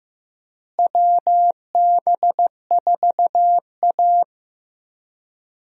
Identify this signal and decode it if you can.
Morse code, heard as a keyed tone, about 15 wpm, 700 Hz. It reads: WB4A
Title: USA Call Signs - 4 Characters - Repeated 3x - 15wpm